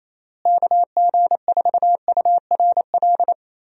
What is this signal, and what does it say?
Morse code, heard as a keyed tone, about 28 wpm, 700 Hz. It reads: KG4URL
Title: USA Call Signs - 6 Characters - Repeated 2x - 28wpm